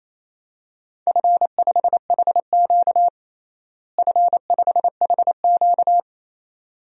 Morse code, heard as a keyed tone, about 28 wpm, 700 Hz. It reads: F5HQ F5HQ
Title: DXCC-100 Call Signs - 4 Characters - Repeated 2x - 28wpm